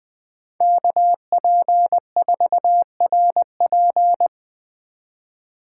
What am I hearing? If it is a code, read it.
Morse code, heard as a keyed tone, about 24 wpm, 700 Hz. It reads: KP4RP
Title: Speed Racing - USA 2x2 Call Signs 20wpm